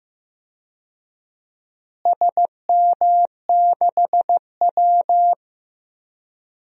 Morse code, heard as a keyed tone, about 15 wpm, 700 Hz. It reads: SM6W
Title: DXCC-100 Call Signs - 4 Characters - Repeated 3x - 15wpm